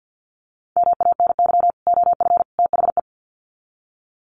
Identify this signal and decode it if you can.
Morse code, heard as a keyed tone, about 50 wpm, 700 Hz. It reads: MANY OF THE
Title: Sets of 3 Words 50wpm